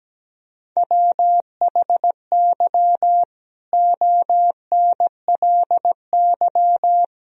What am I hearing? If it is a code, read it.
Morse code, heard as a keyed tone, about 17 wpm, 700 Hz. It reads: WHY ONLY
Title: Sentences from Top 1000 Words - Review Part 4 - 17wpm